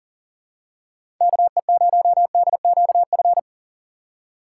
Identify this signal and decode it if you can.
Morse code, heard as a keyed tone, about 37 wpm, 700 Hz. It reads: KE0DQF
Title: USA Call Signs - 6 Characters - 6x Character Spacing - 40wpm